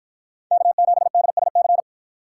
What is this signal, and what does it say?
Morse code, heard as a keyed tone, about 53 wpm, 700 Hz. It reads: K7DSC